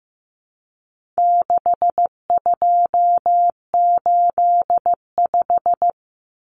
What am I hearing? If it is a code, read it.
Morse code, heard as a keyed tone, about 15 wpm, 700 Hz. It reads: 6285
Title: Common Cut Numbers - 4-Digit 15wpm